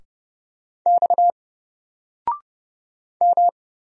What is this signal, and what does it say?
Morse code, heard as a keyed tone, about 30 wpm, 700 Hz. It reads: X M
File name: Single-Letter-Number-30wpm